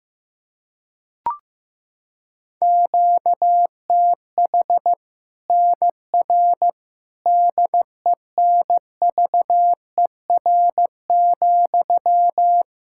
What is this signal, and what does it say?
Morse code, heard as a keyed tone, about 15 wpm, 700 Hz. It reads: QTH NR DENVER,
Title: K1USN SST QSO Practice 15wpm